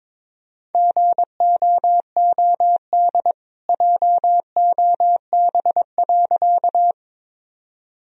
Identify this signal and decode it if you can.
Morse code, heard as a keyed tone, about 21 wpm, 700 Hz. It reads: GOOD JOB.
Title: Sentences from Top 1000 Words - Review Part 3 - 22wpm